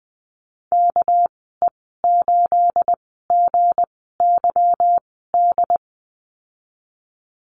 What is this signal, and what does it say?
Morse code, heard as a keyed tone, about 20 wpm, spaced as for 15 wpm, 700 Hz. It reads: KE8GYD
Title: 245 Lesson 39g - Letter z - Call Signs 20wpm